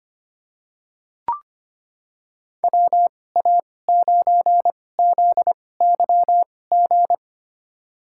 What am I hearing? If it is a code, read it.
Morse code, heard as a keyed tone, about 25 wpm, spaced as for 18 wpm, 700 Hz. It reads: WA9ZYG
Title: USA Call Signs - 6 Characters - 2x Character Spacing - 25wpm